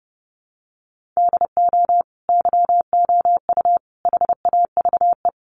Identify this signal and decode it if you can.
Morse code, heard as a keyed tone, about 30 wpm, 700 Hz. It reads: DO YOU HAVE
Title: Sentences from Top 600 Words 30wpm